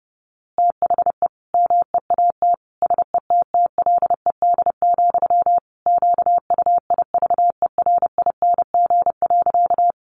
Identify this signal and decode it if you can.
Morse code, heard as a keyed tone, about 30 wpm, 700 Hz. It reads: THE MEAT SETTLED, QUIVERING.
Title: A Sound Of Thunder 30wpm